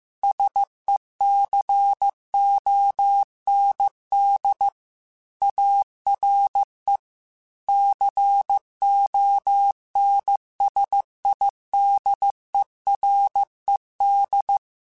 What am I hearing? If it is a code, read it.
Morse code, heard as a keyed tone, about 15 wpm, 800 Hz. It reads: SECOND ARE CONSIDERED